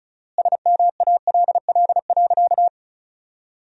Morse code, heard as a keyed tone, about 34 wpm, 700 Hz. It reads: SMALL.
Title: 4x Word-Spacing - Sentences from Top 400 Words - 35wpm